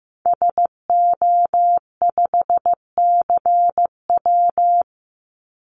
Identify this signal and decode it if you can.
Morse code, heard as a keyed tone, about 15 wpm, 700 Hz. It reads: SO5CW
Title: Call Signs Repeated 2x - 15wpm